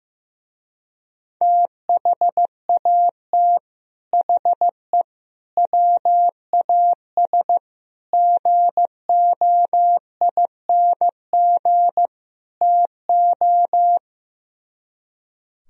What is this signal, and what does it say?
Morse code, heard as a keyed tone, about 16 wpm, 700 Hz. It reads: THAT HE WAS GOING TO
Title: Speed-Racing - Sets of 5 Words 15wpm